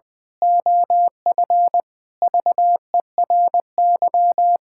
Morse code, heard as a keyed tone, about 20 wpm, 700 Hz. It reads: OF VERY
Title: Pangram Challenge v2 - 20wpm